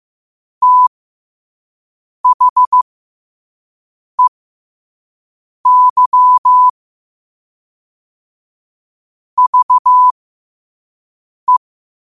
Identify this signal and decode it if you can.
Morse code, heard as a keyed tone, about 15 wpm, spaced as for 5 wpm, 1000 Hz. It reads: THEY VE